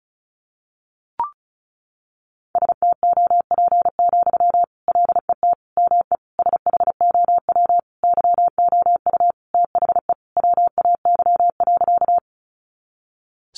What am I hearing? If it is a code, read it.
Morse code, heard as a keyed tone, about 34 wpm, 700 Hz. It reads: STOP, LET ME SHOW YOU THE WAY.
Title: Sentences from Top 300 Words 35wpm